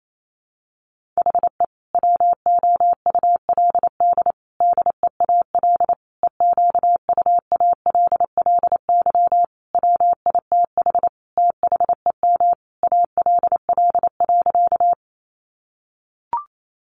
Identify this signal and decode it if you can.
Morse code, heard as a keyed tone, about 28 wpm, 700 Hz. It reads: HE WOULD DEAL EQUALLY WITH THEM ALL.
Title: Sentences from Top 1400 Words 28wpm